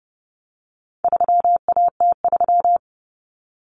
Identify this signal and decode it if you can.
Morse code, heard as a keyed tone, about 30 wpm, 700 Hz. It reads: 3AT3
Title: Common Cut Numbers - 4-Digit 30wpm